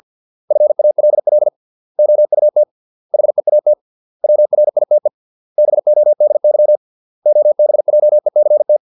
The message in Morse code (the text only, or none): FALL OUT HEAT WRITE BODY OBJECT